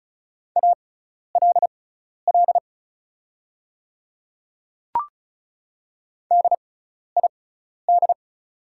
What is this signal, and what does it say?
Morse code, heard as a keyed tone, about 36 wpm, spaced as for 12 wpm, 690 Hz. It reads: ALL DID
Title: 6x Character Spacing - Top 100 Words - 35wpm